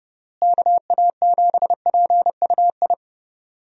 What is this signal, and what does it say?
Morse code, heard as a keyed tone, about 30 wpm, 700 Hz. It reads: KA7PUI
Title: USA Call Signs - 6 Characters - Repeated 3x - 30wpm